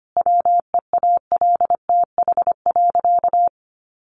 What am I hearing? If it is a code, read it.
Morse code, heard as a keyed tone, about 25 wpm, 700 Hz. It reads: WEALTH.
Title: Sentences from Top 1200 Words 25wpm